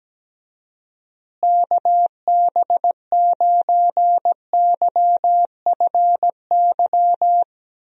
Morse code, heard as a keyed tone, about 17 wpm, 700 Hz. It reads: KB9YFY
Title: USA Call Signs - 6 Characters - Repeated 3x - 17wpm